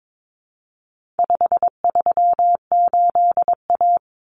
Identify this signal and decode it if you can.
Morse code, heard as a keyed tone, about 22 wpm, 700 Hz. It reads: 538A
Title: Common Cut Numbers - 4-Digit 22wpm